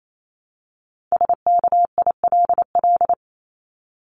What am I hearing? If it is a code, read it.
Morse code, heard as a keyed tone, about 28 wpm, 700 Hz. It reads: SKILL
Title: Top 1001-1250 Words 28wpm